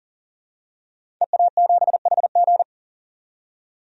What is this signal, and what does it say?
Morse code, heard as a keyed tone, about 40 wpm, 700 Hz. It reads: EA7HG
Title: DXCC-100 Call Signs - 5 Characters - 40wpm